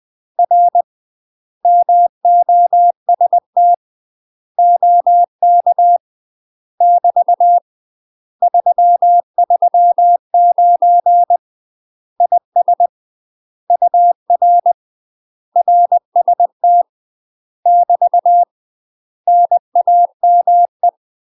Morse code, heard as a keyed tone, about 20 wpm, 700 Hz. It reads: R MOST OK <BT> 339 IS UR RST <BT> NAME
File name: UK MASTER QSOs TRAINING / 20WPM / UK MASTER QSOs 0021